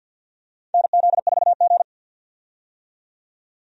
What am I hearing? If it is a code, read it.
Morse code, heard as a keyed tone, about 48 wpm, 700 Hz. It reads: NZ4G
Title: USA Call Signs - 4 Characters - 50wpm